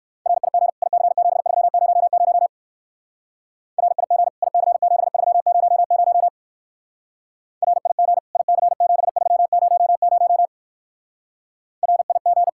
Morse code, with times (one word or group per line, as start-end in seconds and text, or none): 0.3-0.7 RIG
0.8-2.5 IC7300
3.8-4.3 RIG
4.4-6.3 IC7300
7.6-8.2 RIG
8.4-10.5 IC7300
11.8-12.5 RIG